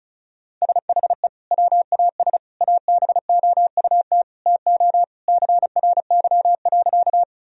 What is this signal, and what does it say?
Morse code, heard as a keyed tone, about 36 wpm, 700 Hz. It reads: SHE WAS ABOUT TO CRY.
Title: Sentences from Top 1000 Words - Review Part 2 - 35wpm